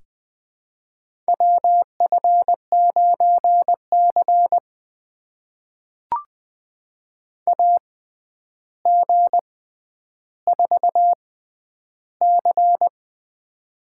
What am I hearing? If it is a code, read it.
Morse code, heard as a keyed tone, about 20 wpm, spaced as for 10 wpm, 700 Hz. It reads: WF9C AG4C
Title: USA Call Signs - 4 Characters - 6x Character Spacing - 20wpm